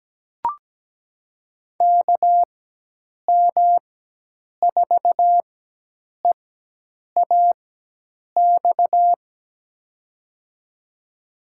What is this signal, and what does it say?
Morse code, heard as a keyed tone, about 17 wpm, spaced as for 8 wpm, 700 Hz. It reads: KM4EAX
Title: USA Call Signs - 6 Characters - 4x Character Spacing - 17wpm